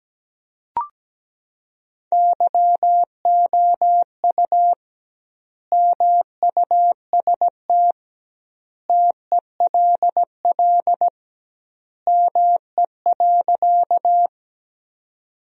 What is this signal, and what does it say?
Morse code, heard as a keyed tone, about 17 wpm, 700 Hz. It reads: YOU MUST TELL ME.
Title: 2x Word-Spacing - Sentences from Top 200 Words - 17wpm